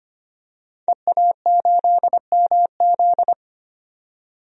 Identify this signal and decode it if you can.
Morse code, heard as a keyed tone, about 25 wpm, 700 Hz. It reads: EA8MZ
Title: DXCC-100 Call Signs - 5 Characters - 25wpm